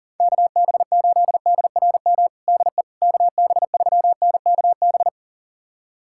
Morse code, heard as a keyed tone, about 37 wpm, 700 Hz. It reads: KB8DRM DE KB3NKB
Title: USA Callsign DE Callsign - 6 Characters - 40wpm